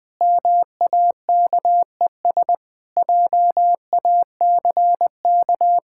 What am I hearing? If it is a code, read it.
Morse code, heard as a keyed tone, about 20 wpm, 700 Hz. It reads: MAKES JACK
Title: Sentences with 8-Word Phrases - Spoken Beginning - 20wpm